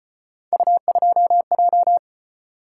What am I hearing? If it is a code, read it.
Morse code, heard as a keyed tone, about 34 wpm, 700 Hz. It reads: U2J